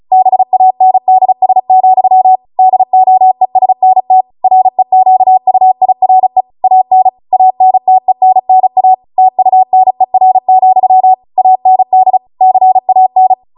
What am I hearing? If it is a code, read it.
Morse code, heard as a keyed tone, about 36 wpm, 750 Hz. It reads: BANDS, DOESNT REQUIRE AN ANTENNA TUNER, AND CAN